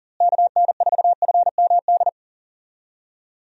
Morse code, heard as a keyed tone, about 38 wpm, 700 Hz. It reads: KN4FMD
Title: USA Call Signs - 6 Characters - Repeated 3x - 40wpm